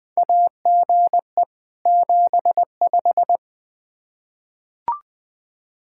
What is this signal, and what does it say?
Morse code, heard as a keyed tone, about 20 wpm, 700 Hz. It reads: AGE 75
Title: K1USN SST QSO Practice 20wpm